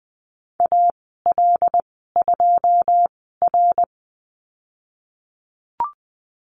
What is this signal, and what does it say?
Morse code, heard as a keyed tone, about 20 wpm, spaced as for 15 wpm, 700 Hz. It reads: AL2R
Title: 082 Lesson 14g - Number 2 - Call Signs 20wpm